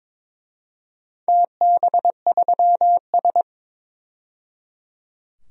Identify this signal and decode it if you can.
Morse code, heard as a keyed tone, about 22 wpm, 700 Hz. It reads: TB3S
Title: DXCC-100 Call Signs - 4 Characters - 22wpm